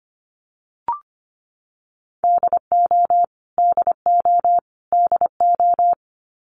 Morse code, heard as a keyed tone, about 25 wpm, 700 Hz. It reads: DO DO DO
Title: Top 200 Words Repeated 25wpm